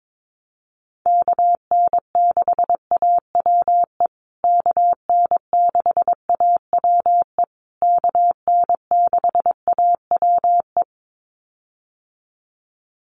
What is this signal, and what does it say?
Morse code, heard as a keyed tone, about 22 wpm, 700 Hz. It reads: KN6AWE KN6AWE KN6AWE
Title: USA Call Signs Repeated 3x 22wpm